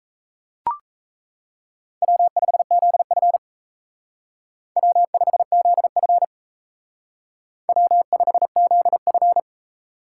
Morse code, heard as a keyed tone, about 32 wpm, 700 Hz. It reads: W5ZF W5ZF W5ZF
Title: Speed Racing - USA 1x2 Call Signs 28wpm